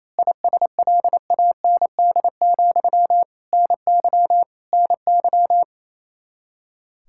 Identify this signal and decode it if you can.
Morse code, heard as a keyed tone, about 27 wpm, 700 Hz. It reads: ISLAND, NY NY
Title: Practice QSO - Exchange 1 - Beginning - 28wpm